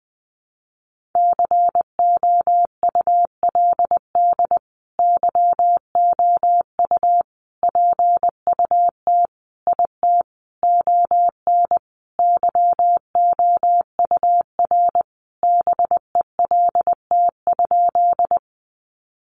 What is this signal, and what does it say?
Morse code, heard as a keyed tone, about 20 wpm, 700 Hz. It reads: COULD YOU PUT IT ON YOUR BELT?
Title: Sentences from Top 1400 Words 20wpm